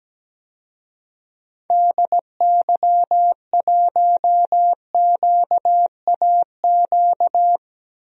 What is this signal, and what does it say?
Morse code, heard as a keyed tone, about 17 wpm, 700 Hz. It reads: DY1QAQ
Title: DXCC-100 Call Signs - 6 Characters - Repeated 2x - 17wpm